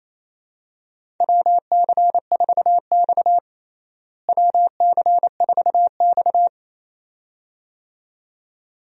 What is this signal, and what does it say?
Morse code, heard as a keyed tone, about 27 wpm, 700 Hz. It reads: WC4X WC4X
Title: USA Call Signs - 4 Characters - Repeated 2x - 28wpm